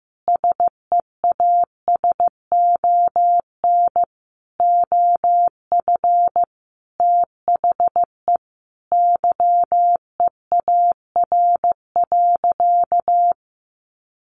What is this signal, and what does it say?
Morse code, heard as a keyed tone, about 15 wpm, 700 Hz. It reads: SEASON OF THE YEAR.
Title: Sentences from Top 1400 Words 15wpm